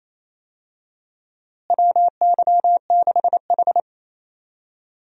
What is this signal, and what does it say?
Morse code, heard as a keyed tone, about 28 wpm, 700 Hz. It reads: WY6H
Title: USA Call Signs - 4 Characters - Repeated 3x - 28wpm